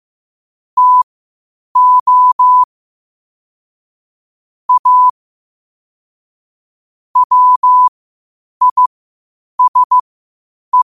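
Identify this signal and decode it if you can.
Morse code, heard as a keyed tone, about 15 wpm, spaced as for 8 wpm, 1000 Hz. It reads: TO A WISE
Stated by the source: Morse code quotes at 8 words per minute from QOTD at 8 WPM.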